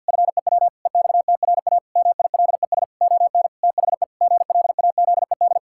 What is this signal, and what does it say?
Morse code, heard as a keyed tone, about 52 wpm, 700 Hz. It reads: FEW EXTRA MILES ON THE GLAZED
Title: Pangram Challenge v2 - 50wpm